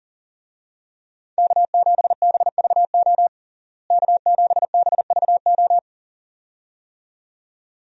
Morse code, heard as a keyed tone, about 40 wpm, 700 Hz. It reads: K7BVO K7BVO
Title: USA Call Signs - 5 Characters - Repeated 2x - 40wpm